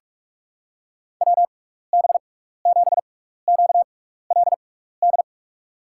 Morse code, heard as a keyed tone, about 48 wpm, spaced as for 16 wpm, 700 Hz. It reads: WB7QLD
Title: USA Call Signs - 6 Characters - 6x Character Spacing - 45wpm